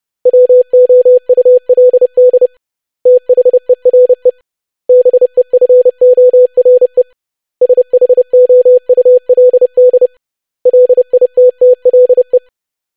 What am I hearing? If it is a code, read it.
Morse code, heard as a keyed tone, about 30 wpm, 500 Hz. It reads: WOULD THERE BEFORE SHOULD LITTLE
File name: lesson / web / basic100-25wpm